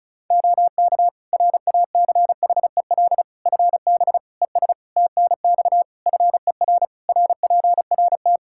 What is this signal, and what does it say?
Morse code, heard as a keyed tone, about 36 wpm, 700 Hz. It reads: OK RACHEL FB ES TNX FER RPRT
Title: K1USN SST QSO Practice 35wpm